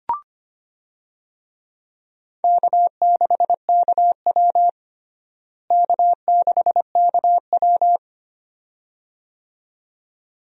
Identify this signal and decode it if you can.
Morse code, heard as a keyed tone, about 25 wpm, 700 Hz. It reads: K6KW K6KW
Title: USA Call Signs - 4 Characters - Repeated 2x - 25wpm